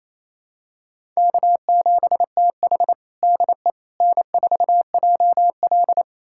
Morse code, heard as a keyed tone, about 28 wpm, 700 Hz. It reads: K7TH DE N4JL
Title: USA Callsign DE Callsign - 4 Characters - 28wpm